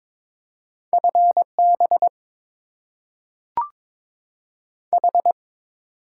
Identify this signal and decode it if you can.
Morse code, heard as a keyed tone, about 22 wpm, 700 Hz. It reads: F B H
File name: Top 20 QSO Elements - 8x Character Spacing - 22wpm